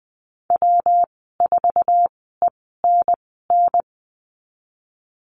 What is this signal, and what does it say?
Morse code, heard as a keyed tone, about 20 wpm, spaced as for 15 wpm, 700 Hz. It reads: W4ENN
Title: 147 Lesson 24g - Letter y - Call Signs 20wpm